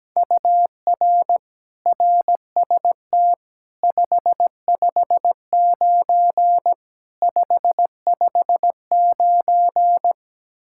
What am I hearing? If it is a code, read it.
Morse code, heard as a keyed tone, about 17 wpm, 700 Hz. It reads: UR RST 559 559
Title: Practice QSO - Exchange 1 - Beginning - 17wpm